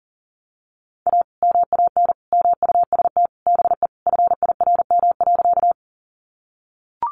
Morse code, heard as a keyed tone, about 40 wpm, 700 Hz. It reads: A MAN MUST BE FIRM.
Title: Sentences from Top 1400 Words 40wpm